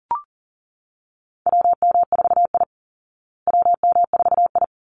Morse code, heard as a keyed tone, about 39 wpm, 700 Hz. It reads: WM4I WM4I
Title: Call Signs Repeated 3x - 40wpm